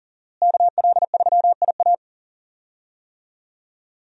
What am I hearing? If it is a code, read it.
Morse code, heard as a keyed tone, about 37 wpm, 700 Hz. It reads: KL3IA
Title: USA Call Signs - 5 Characters - Repeated 3x - 40wpm